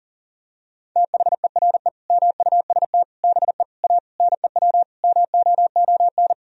Morse code, heard as a keyed tone, about 40 wpm, 700 Hz. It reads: THERE MUST BE A NEW MOON
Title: Sentences from Top 1000 Words - Review Part 3 - 40wpm